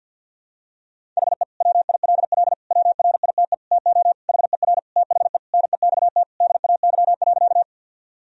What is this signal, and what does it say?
Morse code, heard as a keyed tone, about 46 wpm, 700 Hz. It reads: HE WILL WRITE TO HER THE NEXT DAY.
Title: Sentences from Top 1000 Words - Review Part 1 - 50wpm